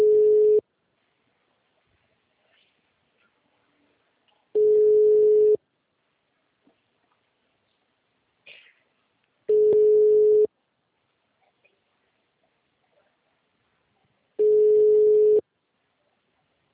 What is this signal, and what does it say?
Ringback tone (Europe, 425 Hz).